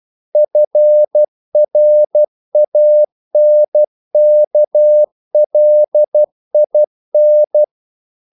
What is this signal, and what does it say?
Morse code, heard as a keyed tone, about 12 wpm, 600 Hz. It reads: FRANKLIN
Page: English quotations in CW (telegraphy - morse)